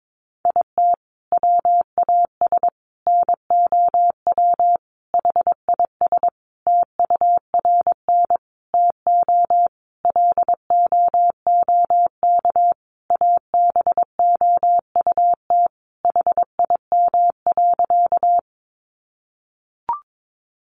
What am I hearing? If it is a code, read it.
Morse code, heard as a keyed tone, about 22 wpm, 700 Hz. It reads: IT WAS NOW HIS TURN TO LOOK ABOUT HIM.
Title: Sentences from Top 200 Words 22wpm